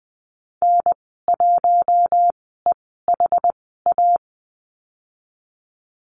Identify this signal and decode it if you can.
Morse code, heard as a keyed tone, about 20 wpm, spaced as for 14 wpm, 700 Hz. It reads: N1EHA
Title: 069 Lesson 12g - Letter d - Call Signs 20wpm